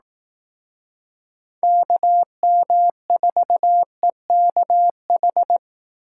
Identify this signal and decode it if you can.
Morse code, heard as a keyed tone, about 18 wpm, 700 Hz. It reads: KM4EKH
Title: Speed Racing - USA Call Signs 15wpm